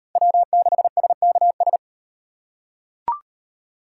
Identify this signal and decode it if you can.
Morse code, heard as a keyed tone, about 38 wpm, 700 Hz. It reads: W6SKS